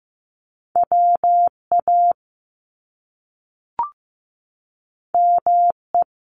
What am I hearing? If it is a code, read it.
Morse code, heard as a keyed tone, about 15 wpm, 700 Hz. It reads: WA ME
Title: Top Two Letter Combinations in Words 15wpm